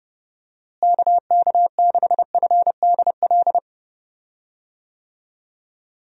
Morse code, heard as a keyed tone, about 30 wpm, 700 Hz. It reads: KK6FDL
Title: USA Call Signs - 6 Characters - Repeated 3x - 30wpm